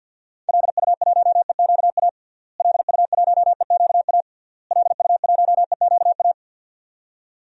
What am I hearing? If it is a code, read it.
Morse code, heard as a keyed tone, about 51 wpm, 700 Hz. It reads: LU1EQA LU1EQA LU1EQA
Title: DXCC-100 Call Signs - 6 Characters - Repeated 3x - 50wpm